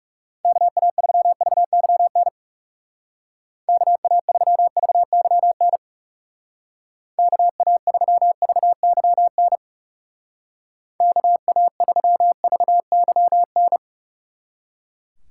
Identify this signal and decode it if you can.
Morse code, heard as a keyed tone, about 38 wpm, 700 Hz. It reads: KA3VYN KA3VYN KA3VYN KA3VYN